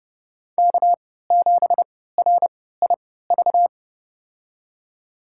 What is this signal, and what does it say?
Morse code, heard as a keyed tone, about 30 wpm, spaced as for 17 wpm, 700 Hz. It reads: K7RIV
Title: USA Call Signs - 5 Characters - 3x Character Spacing - 30wpm